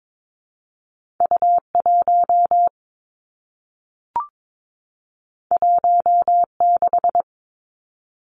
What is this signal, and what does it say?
Morse code, heard as a keyed tone, about 22 wpm, 700 Hz. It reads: U 1 1 6
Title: Two Challenging Letter-Number Combinations 22wpm